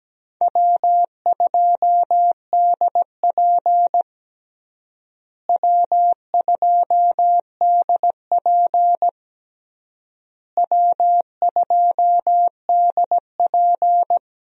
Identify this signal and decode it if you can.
Morse code, heard as a keyed tone, about 17 wpm, 700 Hz. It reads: W2DP W2DP W2DP
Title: USA Call Signs - 4 Characters - Repeated 3x - 17wpm